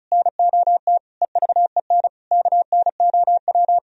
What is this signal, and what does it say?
Morse code, heard as a keyed tone, about 45 wpm, 700 Hz. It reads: NOT EVEN KNOW